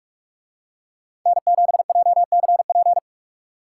Morse code, heard as a keyed tone, about 47 wpm, 700 Hz. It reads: N7JCP